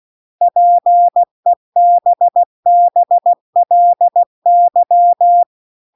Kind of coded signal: Morse code